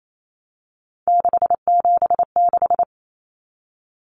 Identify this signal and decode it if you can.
Morse code, heard as a keyed tone, about 28 wpm, 700 Hz. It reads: 676
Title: Numbers - 3-Digit 28wpm